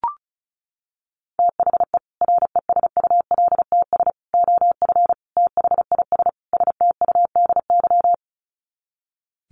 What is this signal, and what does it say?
Morse code, heard as a keyed tone, about 34 wpm, 700 Hz. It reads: THE RESULTS OF THIS STUDY
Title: Sets of 5 Words 35wpm